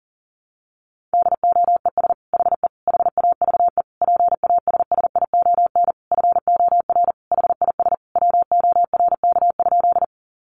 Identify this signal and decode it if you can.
Morse code, heard as a keyed tone, about 40 wpm, 700 Hz. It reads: DOES HE HAVE PASSION FOR HIS WORK?
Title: Sentences from Top 1300 Words 40wpm